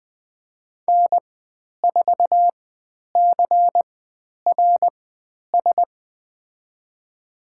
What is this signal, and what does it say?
Morse code, heard as a keyed tone, about 20 wpm, spaced as for 10 wpm, 700 Hz. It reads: N4CRS
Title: 102 Lesson 17g - Letter u - Call Signs 20wpm